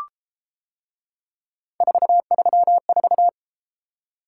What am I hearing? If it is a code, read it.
Morse code, heard as a keyed tone, about 33 wpm, 700 Hz. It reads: 434